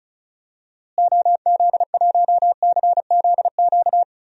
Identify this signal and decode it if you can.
Morse code, heard as a keyed tone, about 34 wpm, 700 Hz. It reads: OZ1CZQ
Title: DXCC-100 Call Signs - 6 Characters - Repeated 2x - 35wpm